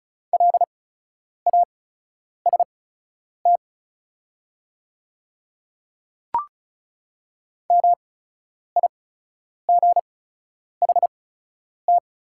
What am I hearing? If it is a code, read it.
Morse code, heard as a keyed tone, about 34 wpm, spaced as for 9 wpm, 700 Hz. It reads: LAST MIGHT
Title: 8x Character Spacing - Top 300 Words - 35wpm